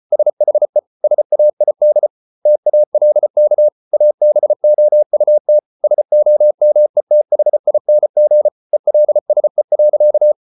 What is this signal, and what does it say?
Morse code, heard as a keyed tone, about 34 wpm, 600 Hz. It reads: SHE SAID TALK ABOUT SOMETHING ELSE.